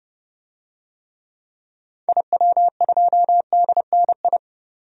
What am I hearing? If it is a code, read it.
Morse code, heard as a keyed tone, about 32 wpm, 700 Hz. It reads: IW2DNI
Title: DXCC-100 Call Signs - 6 Characters - 30wpm